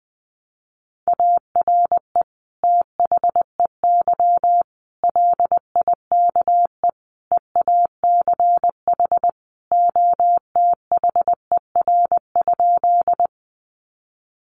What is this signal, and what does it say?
Morse code, heard as a keyed tone, about 20 wpm, 700 Hz. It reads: ARE THEY LIKE EACH OTHER?
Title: Sentences from Top 200 Words 20wpm